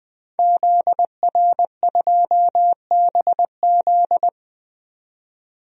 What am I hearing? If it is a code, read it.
Morse code, heard as a keyed tone, about 20 wpm, 700 Hz. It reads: ZR2BZ
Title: DXCC-100 Call Signs - 5 Characters - 20wpm